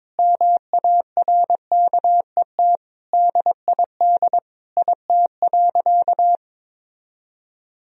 Morse code, heard as a keyed tone, about 22 wpm, 700 Hz. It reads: MARKET DID IT.
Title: Sentences from Top 1000 Words - Review Part 3 - 22wpm